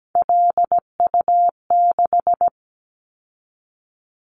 Morse code, heard as a keyed tone, about 17 wpm, 700 Hz. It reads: LU6
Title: Callsign Prefixes for Top 100 DXCC - 17wpm